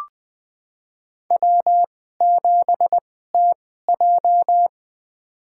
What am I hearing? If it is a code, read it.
Morse code, heard as a keyed tone, about 20 wpm, spaced as for 14 wpm, 700 Hz. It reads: W7TJ
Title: USA Call Signs - 2x Character Spacing - 20wpm